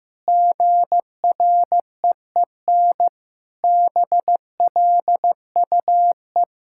Morse code, heard as a keyed tone, about 14 wpm, 700 Hz. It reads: GREEN BLUE
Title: Pangram Challenge v2 - 15wpm